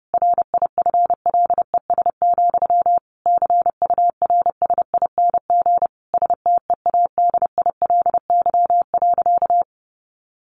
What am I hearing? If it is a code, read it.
Morse code, heard as a keyed tone, about 30 wpm, 700 Hz. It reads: RIFLES, CURSING STEADILY.
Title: A Sound Of Thunder 30wpm